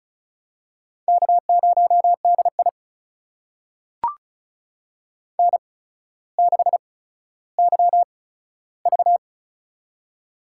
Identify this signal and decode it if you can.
Morse code, heard as a keyed tone, about 36 wpm, spaced as for 15 wpm, 700 Hz. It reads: K0DI N6YV
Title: USA Call Signs - 4 Characters - 8x Character Spacing - 35wpm